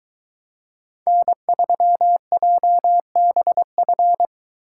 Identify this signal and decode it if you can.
Morse code, heard as a keyed tone, about 22 wpm, 700 Hz. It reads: N3JBF